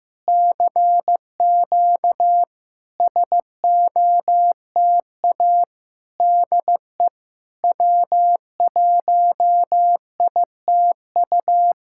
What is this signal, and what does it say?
Morse code, heard as a keyed tone, about 16 wpm, 700 Hz. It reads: CQ SOTA DE W1ITU
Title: SOTA QSO Practice 15wpm